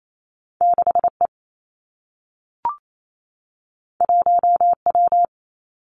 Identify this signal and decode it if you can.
Morse code, heard as a keyed tone, about 28 wpm, 700 Hz. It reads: 6 E 1 W
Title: Two Challenging Letter-Number Combinations 28wpm